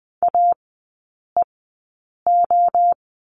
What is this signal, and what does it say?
Morse code, heard as a keyed tone, about 20 wpm, spaced as for 8 wpm, 700 Hz. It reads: AEO
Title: 041 Lesson 8e - Number 1 - Three Characters 20wpm